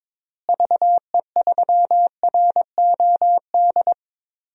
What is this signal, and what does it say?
Morse code, heard as a keyed tone, about 22 wpm, 700 Hz. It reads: VE3ROD
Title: DXCC-100 Call Signs - 6 Characters - Repeated 3x - 22wpm